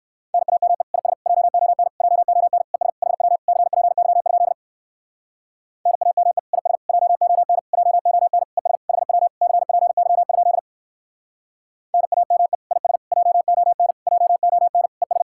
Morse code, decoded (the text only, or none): NAME IS JON JON ES HW CPY? NAME IS JON JON ES HW CPY? NAME IS JON JON ES